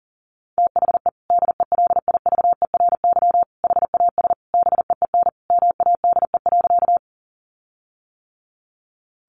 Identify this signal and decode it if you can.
Morse code, heard as a keyed tone, about 37 wpm, 700 Hz. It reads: THE DELIVERY HAS BEEN MADE.
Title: Sentences from Top 1500 Words 40wpm